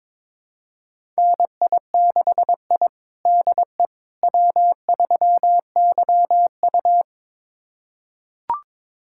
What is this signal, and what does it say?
Morse code, heard as a keyed tone, about 22 wpm, 700 Hz. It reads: NI6I DE W3YU
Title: USA Callsign DE Callsign - 4 Characters - 22wpm